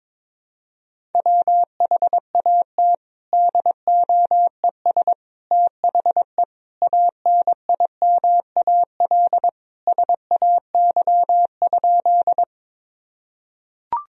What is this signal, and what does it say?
Morse code, heard as a keyed tone, about 22 wpm, 700 Hz. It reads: WHAT DOES THE ANIMAL SAY?
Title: Sentences from Top 1000 Words - Review Part 1 - 22wpm